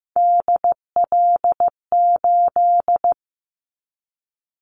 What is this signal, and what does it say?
Morse code, heard as a keyed tone, about 15 wpm, 700 Hz. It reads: DL8
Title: Callsign Prefixes for Top 25 DXCC - 15wpm